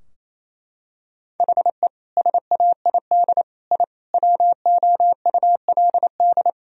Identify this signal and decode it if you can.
Morse code, heard as a keyed tone, about 28 wpm, 700 Hz. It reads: HE SAID I WOULD
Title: Sentences from Top 1000 Words - Review Part 4 - 28wpm